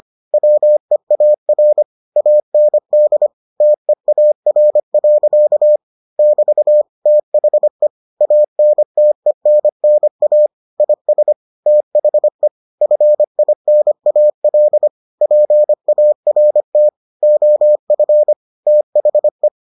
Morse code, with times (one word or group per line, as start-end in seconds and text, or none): 0.3-1.8 WEAR
2.2-3.3 AND
3.6-5.8 TEAR.
6.2-6.8 <BT>
7.1-7.9 THE
8.2-10.5 ANTENNA
10.8-11.3 IS
11.7-12.5 THE
12.8-14.9 FINAL
15.2-16.9 PART
17.2-18.3 OF
18.7-19.5 THE